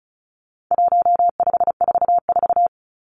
Morse code, heard as a keyed tone, about 34 wpm, 700 Hz. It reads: 1544
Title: Numbers - 4-Digit 35wpm